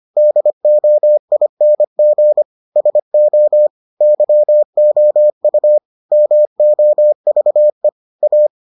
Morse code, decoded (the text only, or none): DOING SO YOU MOVE A